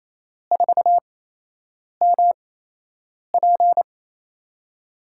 Morse code, heard as a keyed tone, about 27 wpm, spaced as for 8 wpm, 700 Hz. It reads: 4MP
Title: USA Call Signs - 4 Characters - 8x Character Spacing - 28wpm